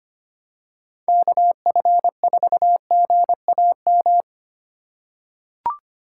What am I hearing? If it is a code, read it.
Morse code, heard as a keyed tone, about 25 wpm, 700 Hz. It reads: KF4GAM
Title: USA Call Signs - 2x Character Spacing - 25wpm